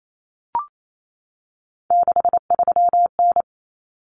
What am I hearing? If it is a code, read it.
Morse code, heard as a keyed tone, about 28 wpm, 700 Hz. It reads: 63N
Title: Common Cut Numbers - 3-Digit 28wpm